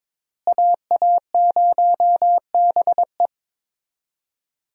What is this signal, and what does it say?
Morse code, heard as a keyed tone, about 22 wpm, 700 Hz. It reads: AA0BE
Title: USA Call Signs - 5 Characters - Repeated 3x - 22wpm